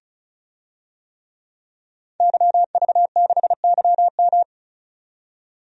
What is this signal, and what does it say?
Morse code, heard as a keyed tone, about 36 wpm, 700 Hz. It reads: YV6YM
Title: DXCC-100 Call Signs - 5 Characters - 35wpm